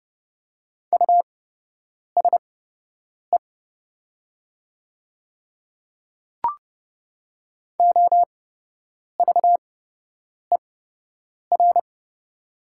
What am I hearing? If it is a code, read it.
Morse code, heard as a keyed tone, about 29 wpm, spaced as for 8 wpm, 700 Hz. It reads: USE OVER
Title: 8x Character Spacing - Top 100 Words - 30wpm